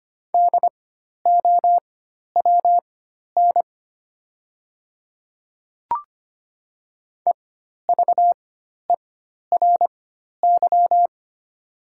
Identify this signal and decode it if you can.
Morse code, heard as a keyed tone, about 25 wpm, spaced as for 12 wpm, 700 Hz. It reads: DOWN EVERY
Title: 4x Character Spacing - Top 500 Words - 25wpm